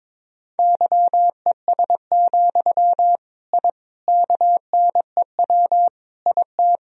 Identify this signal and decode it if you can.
Morse code, heard as a keyed tone, about 22 wpm, 700 Hz. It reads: YES, I KNEW IT
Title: Sentences from Top 1000 Words - Review Part 2 - 22wpm